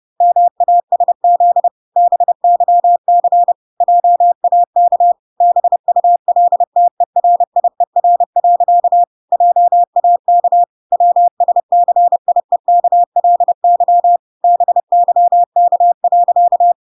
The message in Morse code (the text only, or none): MASZ BYC JAK BULTERIER. JAK WSCIEKLY BYK.